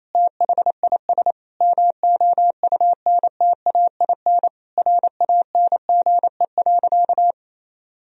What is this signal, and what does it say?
Morse code, heard as a keyed tone, about 28 wpm, 700 Hz. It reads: THIS MOUNTAIN RANGE.
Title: Sentences from Top 1000 Words - Review Part 4 - 28wpm